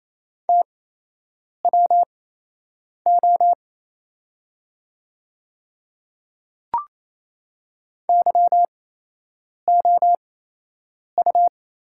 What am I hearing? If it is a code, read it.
Morse code, heard as a keyed tone, about 27 wpm, spaced as for 8 wpm, 690 Hz. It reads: TWO YOU
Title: 8x Character Spacing - Top 100 Words - 28wpm